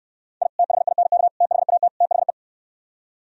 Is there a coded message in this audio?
Yes: Morse code from 0.4 to 2.3 s, heard as a keyed tone, about 67 wpm, 700 Hz. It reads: I THINK THAT THE